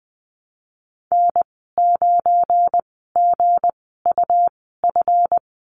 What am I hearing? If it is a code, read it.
Morse code, heard as a keyed tone, about 20 wpm, spaced as for 15 wpm, 700 Hz. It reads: N9GUF
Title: 179 Lesson 29g - Slash - Call Signs 20wpm